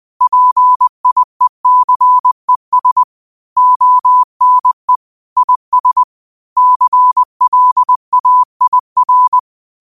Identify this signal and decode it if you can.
Morse code, heard as a keyed tone, about 20 wpm, 1000 Hz. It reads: PIECES ONE IS CLAIR